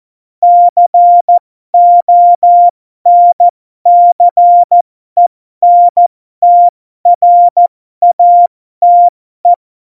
Morse code, heard as a keyed tone, about 14 wpm, spaced as for 12 wpm, 700 Hz. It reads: CONCENTRATE
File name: Morsekurs / Witze